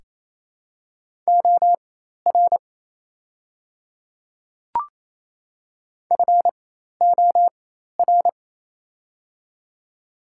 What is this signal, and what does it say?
Morse code, heard as a keyed tone, about 28 wpm, spaced as for 13 wpm, 700 Hz. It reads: OR FOR
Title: 4x Character Spacing - Top 100 Words - 28wpm